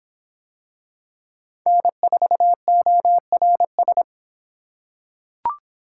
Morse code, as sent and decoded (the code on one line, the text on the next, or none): -. ....- --- .-. ...
N4ORS